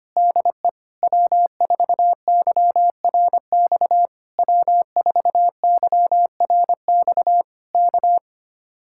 Morse code, heard as a keyed tone, about 24 wpm, 700 Hz. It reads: DE W4YRX W4YRX K
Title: Speed-Racing Common QSO Phrases - 25wpm